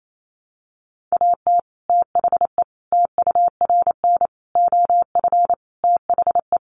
Morse code, heard as a keyed tone, about 28 wpm, 700 Hz. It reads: AT THE TURN OF THE
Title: Sets of 5 Words 28wpm